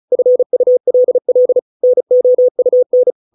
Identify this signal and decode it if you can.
Morse code, heard as a keyed tone, about 34 wpm, 500 Hz. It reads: FULL NOUN